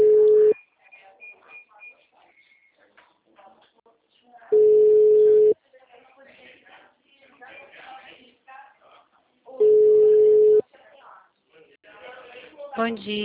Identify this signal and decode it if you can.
Ringback tone (Europe, 425 Hz), with speech.